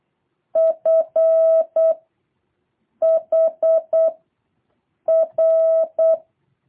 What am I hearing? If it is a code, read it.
Morse code, heard as a keyed tone, about 8 wpm, 650 Hz. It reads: F H R
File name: jh8cbh / 2023cw